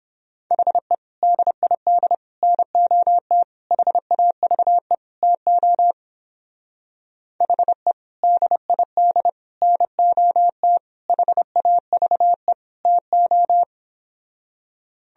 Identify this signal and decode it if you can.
Morse code, heard as a keyed tone, about 31 wpm, 700 Hz. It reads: HE DID NOT HAVE TO HE DID NOT HAVE TO